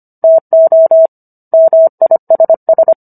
Morse code, heard as a keyed tone, about 25 wpm, 650 Hz. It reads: TO MISS